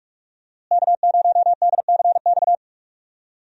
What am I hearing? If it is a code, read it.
Morse code, heard as a keyed tone, about 46 wpm, 700 Hz. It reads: K0DCX
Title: USA Call Signs - 5 Characters - Repeated 2x - 45wpm